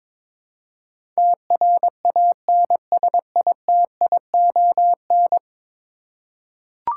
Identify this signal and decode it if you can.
Morse code, heard as a keyed tone, about 22 wpm, 700 Hz. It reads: TRANSITION
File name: practice / 22wpm / 10 Letter Words - First 3 Letters Given - 22wpm